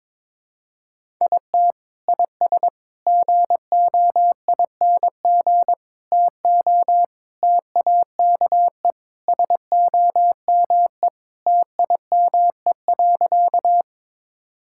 Morse code, heard as a keyed tone, about 22 wpm, 700 Hz. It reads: IT IS GOING TO TAKE SOME TIME.
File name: practice / 22wpm / Sentences with 3-Word Phrases - 22wpm